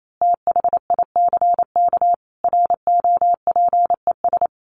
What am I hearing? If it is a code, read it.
Morse code, heard as a keyed tone, about 28 wpm, 700 Hz. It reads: THICK ROPES
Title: A Sound Of Thunder 28wpm